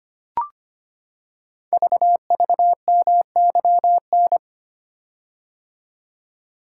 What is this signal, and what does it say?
Morse code, heard as a keyed tone, about 25 wpm, 700 Hz. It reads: VVMYN
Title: Code Groups - 5 Letters - 25wpm